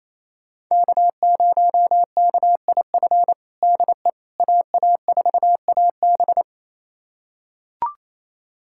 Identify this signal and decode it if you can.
Morse code, heard as a keyed tone, about 27 wpm, 700 Hz. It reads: K0KIF DE AA4AB
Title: USA Callsign DE Callsign - 5 Characters - 28wpm